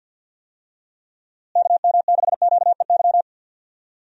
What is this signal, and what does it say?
Morse code, heard as a keyed tone, about 54 wpm, 700 Hz. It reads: KM6QEY
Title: USA Call Signs - 6 Characters - Repeated 2x - 50wpm